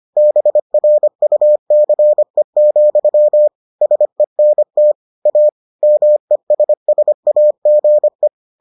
Morse code, heard as a keyed tone, about 25 wpm, 600 Hz. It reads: BRUCE, SENT A MESSAGE